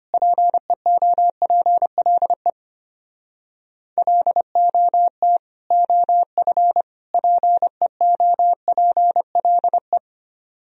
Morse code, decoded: PEOPLE LOT OF PEOPLE

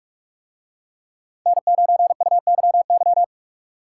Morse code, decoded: N9UYY

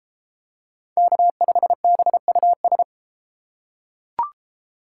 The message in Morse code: K5BUS

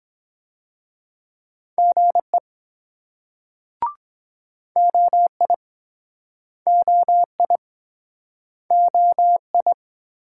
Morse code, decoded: GE OI OI OI

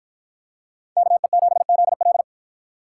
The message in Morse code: KE7ZL

